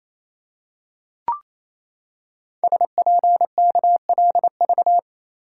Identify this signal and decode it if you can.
Morse code, heard as a keyed tone, about 27 wpm, 700 Hz. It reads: SPKLV
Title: Code Groups - 5 Letters - 28wpm